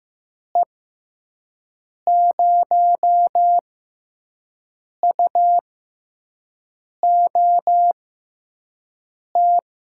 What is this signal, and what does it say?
Morse code, heard as a keyed tone, about 15 wpm, spaced as for 5 wpm, 700 Hz. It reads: E0UOT
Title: USA Call Signs - 6 Characters - 6x Character Spacing - 15wpm